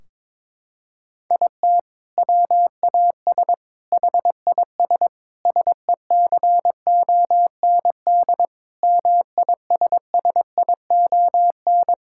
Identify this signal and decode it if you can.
Morse code, heard as a keyed tone, about 22 wpm, 700 Hz. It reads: IT WAS HIS SECOND MISSION
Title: Sentences from Top 1600 Words 22wpm